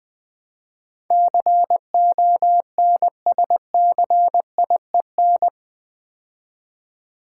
Morse code, heard as a keyed tone, about 20 wpm, 700 Hz. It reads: CONSCIEN
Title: Build A Word - 4 Letters At A Time - 10-Letter Words - 20wpm